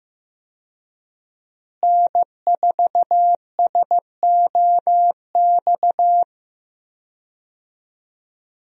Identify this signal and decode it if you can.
Morse code, heard as a keyed tone, about 15 wpm, 700 Hz. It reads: N4SOX
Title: USA Call Signs - 5 Characters - Repeated 3x - 15wpm